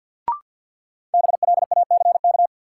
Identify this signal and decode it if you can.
Morse code, heard as a keyed tone, about 47 wpm, 700 Hz. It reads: BLACK
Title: Sets of 5 Words Encore - New Word Drill - 50wpm